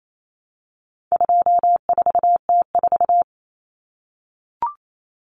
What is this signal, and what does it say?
Morse code, heard as a keyed tone, about 28 wpm, 700 Hz. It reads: 24T4
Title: Common Cut Numbers - 4-Digit 28wpm